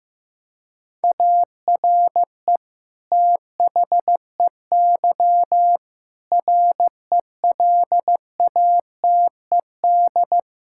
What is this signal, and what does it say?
Morse code, heard as a keyed tone, about 16 wpm, 700 Hz. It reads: ARE THEY RELATED
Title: Sentences from Top 1000 Words - Review Part 5 - 15wpm